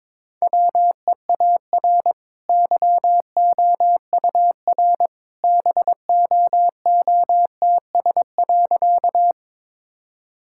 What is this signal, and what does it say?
Morse code, heard as a keyed tone, about 22 wpm, 690 Hz. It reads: WEAR YOUR BOOTS.
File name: practice / 22wpm / Easy Sentences with Hints - 22wpm